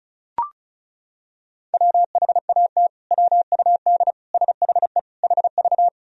Morse code, heard as a keyed tone, about 34 wpm, 700 Hz. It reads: WHAT WUD SHE HV
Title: Sentences from Top 100 Words and Abbreviations 35wpm